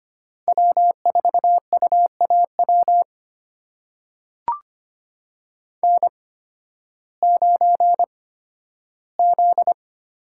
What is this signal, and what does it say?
Morse code, heard as a keyed tone, about 25 wpm, spaced as for 13 wpm, 700 Hz. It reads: W4UAW N9Z
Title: USA Call Signs - 5 Characters - 8x Character Spacing - 25wpm